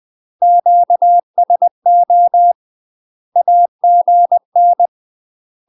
Morse code, heard as a keyed tone, about 20 wpm, 700 Hz. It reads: QSO AGN
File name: UK MASTER QSOs TRAINING / 20WPM / UK MASTER QSOs 0051